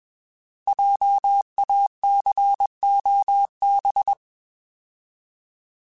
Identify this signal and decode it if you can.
Morse code, heard as a keyed tone, about 21 wpm, 800 Hz. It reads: JACOB
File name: Alphabetical / Alphabetical mp3s / Js @20